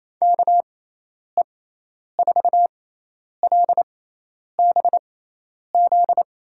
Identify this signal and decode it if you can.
Morse code, heard as a keyed tone, about 28 wpm, spaced as for 10 wpm, 700 Hz. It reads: KE4LBZ
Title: USA Call Signs - 6 Characters - 6x Character Spacing - 28wpm